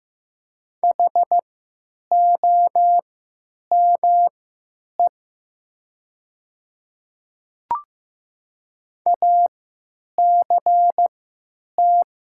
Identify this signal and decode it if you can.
Morse code, heard as a keyed tone, about 15 wpm, spaced as for 9 wpm, 700 Hz. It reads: HOME ACT
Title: 3x Character Spacing - Top 200 Words - 15wpm